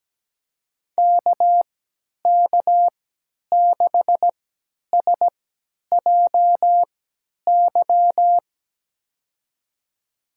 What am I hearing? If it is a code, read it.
Morse code, heard as a keyed tone, about 17 wpm, spaced as for 10 wpm, 700 Hz. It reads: KK6SJY
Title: USA Call Signs - 6 Characters - 3x Character Spacing - 17wpm